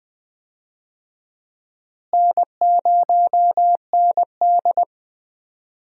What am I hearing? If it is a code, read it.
Morse code, heard as a keyed tone, about 20 wpm, 700 Hz. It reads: N0ND
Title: USA Call Signs - 4 Characters - Repeated 3x - 20wpm